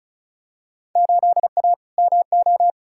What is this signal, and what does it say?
Morse code, heard as a keyed tone, about 36 wpm, 700 Hz. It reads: 8A MO
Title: Field Day Exchanges Encore - 35wpm